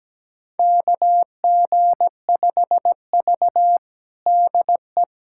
Morse code, heard as a keyed tone, about 17 wpm, 700 Hz. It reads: KG5V DE
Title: USA Callsign DE Callsign - 4 Characters - 17wpm